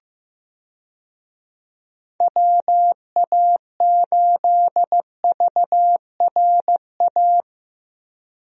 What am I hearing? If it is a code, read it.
Morse code, heard as a keyed tone, about 15 wpm, 700 Hz. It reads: WA8VRA
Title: USA Call Signs - 6 Characters - 15wpm